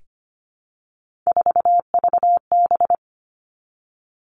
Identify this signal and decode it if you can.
Morse code, heard as a keyed tone, about 25 wpm, 700 Hz. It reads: 4VB
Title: Three Challenging Letter-Number Combinations 25wpm